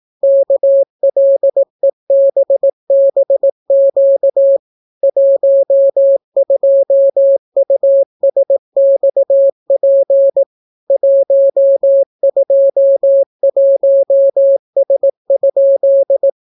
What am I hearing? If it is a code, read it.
Morse code, heard as a keyed tone, about 18 wpm, 550 Hz. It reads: KLEBBQ 12USXP 121S?